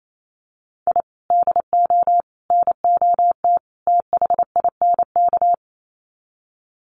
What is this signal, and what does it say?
Morse code, heard as a keyed tone, about 28 wpm, 700 Hz. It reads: I DO NOT THINK
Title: Sets of 4 Words 28wpm